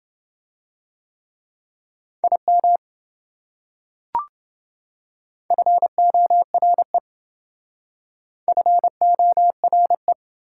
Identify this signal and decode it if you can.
Morse code, heard as a keyed tone, about 28 wpm, 700 Hz. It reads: IM FORE FORE